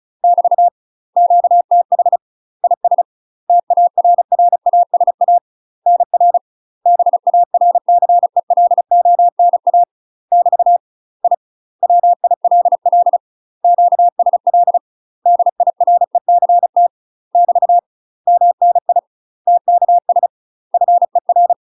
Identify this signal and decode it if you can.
Morse code, heard as a keyed tone, about 34 wpm, 700 Hz. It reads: <BT> QTH IS TARRASA NR BARCELONA <BT> I WILL QSL DIRECT <BT> MNI TKS FER
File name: UK MASTER QSOs TRAINING / 35WPM / UK MASTER QSOs 0042